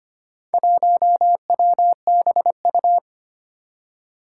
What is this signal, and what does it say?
Morse code, heard as a keyed tone, about 25 wpm, 700 Hz. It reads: 1WBU
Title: Four Challenging Letter-Number Combinations 25wpm